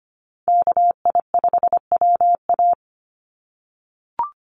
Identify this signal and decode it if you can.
Morse code, heard as a keyed tone, about 25 wpm, 700 Hz. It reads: KI5WA
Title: NA SOTA Chaser Callsigns 25wpm